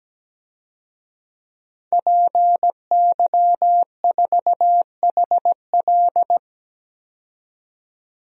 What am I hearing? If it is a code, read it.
Morse code, heard as a keyed tone, about 17 wpm, 700 Hz. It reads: PY4HL
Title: DXCC-100 Call Signs - 5 Characters - 17wpm